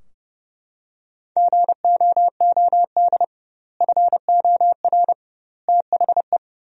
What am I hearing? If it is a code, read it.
Morse code, heard as a keyed tone, about 32 wpm, 700 Hz. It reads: GOOD FOR THE
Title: 2x Word-Spacing - Sentences from Top 300 Words - 30wpm